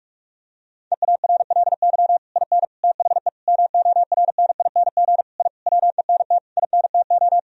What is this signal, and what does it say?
Morse code, heard as a keyed tone, about 42 wpm, 700 Hz. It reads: EARLY IN THE MORNING I WENT INTO
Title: Sentences from Top 1000 Words - Review Part 2 - 45wpm